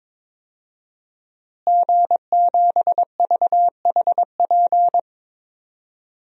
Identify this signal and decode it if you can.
Morse code, heard as a keyed tone, about 21 wpm, 700 Hz. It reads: G7VHP
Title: DXCC-100 Call Signs - 5 Characters - Repeated 2x - 22wpm